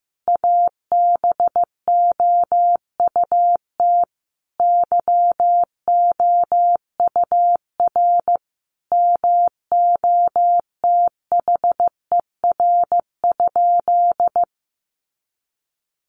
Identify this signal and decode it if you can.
Morse code, heard as a keyed tone, about 15 wpm, 700 Hz. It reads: ABOUT YOUR MOTHER?
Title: Sentences from Top 200 Words 15wpm